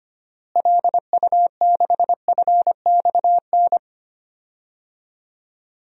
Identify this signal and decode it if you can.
Morse code, heard as a keyed tone, about 25 wpm, 700 Hz. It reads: LU6FXN
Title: DXCC-100 Call Signs - 6 Characters - Repeated 2x - 25wpm